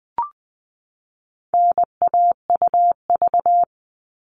Morse code, heard as a keyed tone, about 20 wpm, 700 Hz. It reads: NAUV
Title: All Cut Numbers - 4-Digit 20wpm